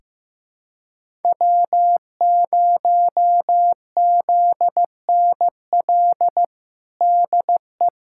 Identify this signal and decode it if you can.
Morse code, heard as a keyed tone, about 16 wpm, 700 Hz. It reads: W0ZNL DE
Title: USA Callsign DE Callsign 15wpm